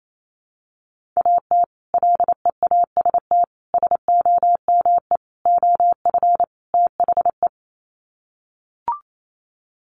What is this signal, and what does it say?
Morse code, heard as a keyed tone, about 28 wpm, 700 Hz. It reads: AT LEAST SOME OF THE
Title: Sets of 5 Words Encore 28wpm